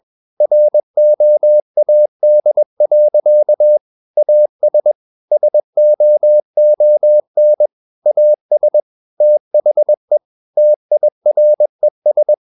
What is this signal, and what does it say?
Morse code, heard as a keyed tone, about 21 wpm, 600 Hz. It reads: ROAD. AS SOON AS THE TIRES
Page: Romanen Never af Ken Follet på engelsk.